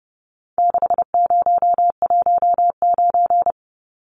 Morse code, heard as a keyed tone, about 30 wpm, 700 Hz. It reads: 6019
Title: Numbers - 4-Digit 30wpm